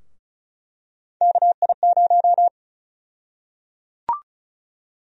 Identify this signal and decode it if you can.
Morse code, heard as a keyed tone, about 36 wpm, 700 Hz. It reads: KI0
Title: USA Call Sign Prefixes - Repeated 2x - 35wpm